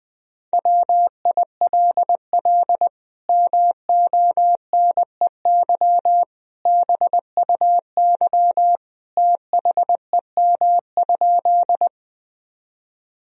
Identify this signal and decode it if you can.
Morse code, heard as a keyed tone, about 21 wpm, 700 Hz. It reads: WILL MONEY BUY THEM?
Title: Sentences from Top 1000 Words - Review Part 4 - 20wpm